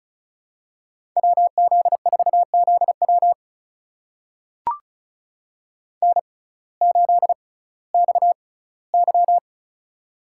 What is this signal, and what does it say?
Morse code, heard as a keyed tone, about 34 wpm, spaced as for 19 wpm, 700 Hz. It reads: WZ4ZW N8XY
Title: USA Call Signs - 6x Character Spacing - 35wpm